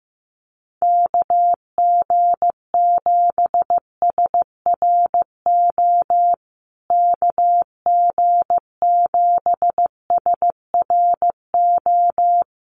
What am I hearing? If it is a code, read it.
Morse code, heard as a keyed tone, about 15 wpm, 700 Hz. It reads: KG7SRO KG7SRO
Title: USA Call Signs Repeated 3x 15wpm